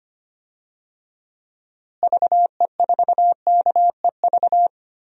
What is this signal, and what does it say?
Morse code, heard as a keyed tone, about 25 wpm, 700 Hz. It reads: VE4KEV
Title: DXCC-100 Call Signs - 6 Characters - Repeated 3x - 25wpm